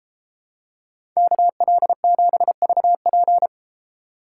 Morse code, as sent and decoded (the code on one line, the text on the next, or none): -.- .-.. --... ...- .--.
KL7VP